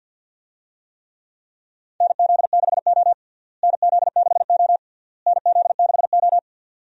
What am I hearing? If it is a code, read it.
Morse code, heard as a keyed tone, about 50 wpm, 700 Hz. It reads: NZ6O NZ6O NZ6O
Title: USA Call Signs - 4 Characters - Repeated 3x - 50wpm